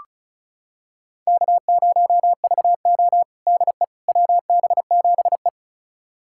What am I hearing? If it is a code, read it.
Morse code, heard as a keyed tone, about 36 wpm, 700 Hz. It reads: K0VO DE WB7E
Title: USA Callsign DE Callsign - 4 Characters - 35wpm